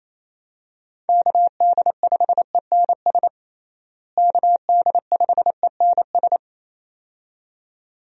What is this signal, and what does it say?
Morse code, heard as a keyed tone, about 28 wpm, 700 Hz. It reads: KD5ENS KD5ENS
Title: USA Call Signs - 6 Characters - Repeated 2x - 28wpm